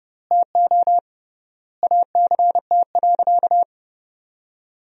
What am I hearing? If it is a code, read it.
Morse code, heard as a keyed tone, about 30 wpm, 700 Hz. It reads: TO ACT.
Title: 3x Word-Spacing - Sentences from Top 200 Words - 30wpm